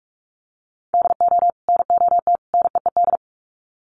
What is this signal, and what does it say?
Morse code, heard as a keyed tone, about 44 wpm, 700 Hz. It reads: DO NOT NEED
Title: Sets of 3 Words 45wpm